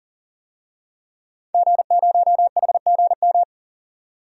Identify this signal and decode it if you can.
Morse code, heard as a keyed tone, about 40 wpm, 700 Hz. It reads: G0HGM
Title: DXCC-100 Call Signs - 5 Characters - 40wpm